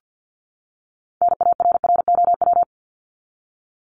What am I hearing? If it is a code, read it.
Morse code, heard as a keyed tone, about 50 wpm, 700 Hz. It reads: NARROW
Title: Top 1501-1750 Words 50wpm